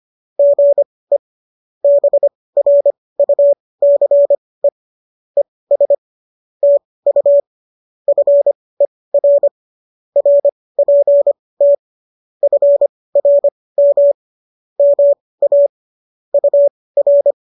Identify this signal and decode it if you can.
Morse code, heard as a keyed tone, about 25 wpm, spaced as for 18 wpm, 570 Hz. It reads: GE BRUCE ES TU FER RPT FRM MA UR